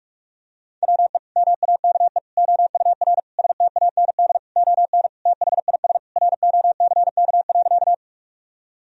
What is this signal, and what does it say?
Morse code, heard as a keyed tone, about 46 wpm, 700 Hz. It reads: WE MAKE OUR STAND ON THIS ROCK.
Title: Sentences from Top 1000 Words - Review Part 2 - 45wpm